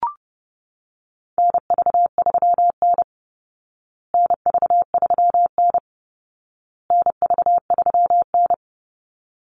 Morse code, heard as a keyed tone, about 30 wpm, 700 Hz. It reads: NV3N NV3N NV3N
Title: Call Signs Repeated 2x - 30wpm